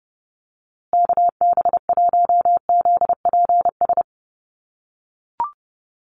Morse code, heard as a keyed tone, about 30 wpm, 700 Hz. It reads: KB1ZPS
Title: USA Call Signs Repeated 2x 30wpm